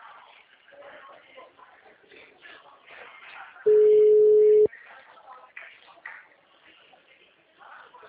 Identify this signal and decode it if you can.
Ringback tone (Europe, 425 Hz), with speech.